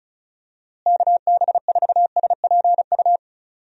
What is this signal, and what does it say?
Morse code, heard as a keyed tone, about 34 wpm, 700 Hz. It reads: KB4SPU
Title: USA Call Signs - 6 Characters - Repeated 3x - 35wpm